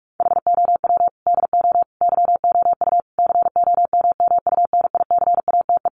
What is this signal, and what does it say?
Morse code, heard as a keyed tone, about 44 wpm, 700 Hz. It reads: HOW DO YOU COMMUNICATE
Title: Sentences from Top 1300 Words 45wpm